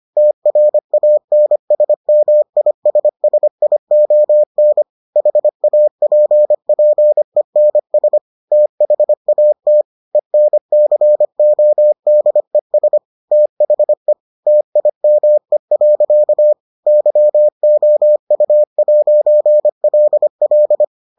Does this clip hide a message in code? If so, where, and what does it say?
Yes: Morse code from 0.2 to 20.8 s, heard as a keyed tone, about 25 wpm, 600 Hz. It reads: TRANSMISSION HAPPENS THAT ENCODES THE TIME. YOU'LL